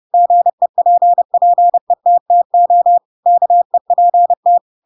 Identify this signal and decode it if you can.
Morse code, heard as a keyed tone, about 30 wpm, 700 Hz. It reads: GEPPETTO KEPT
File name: Pinocchio / Pinocchio ch3 30 30 ik1hge